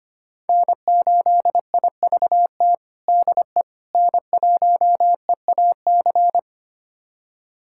Morse code, heard as a keyed tone, about 25 wpm, 700 Hz. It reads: N8IVT DE N1EAC
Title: USA Callsign DE Callsign - 5 Characters - 25wpm